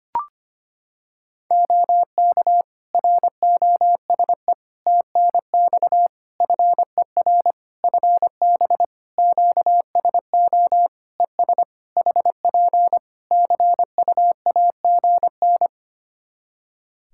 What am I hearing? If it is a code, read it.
Morse code, heard as a keyed tone, about 25 wpm, 700 Hz. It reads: OK ROSE TNX FER FB QSO ES HP CUAGN
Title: Practice QSO - Exchange 4 - Ending - 25wpm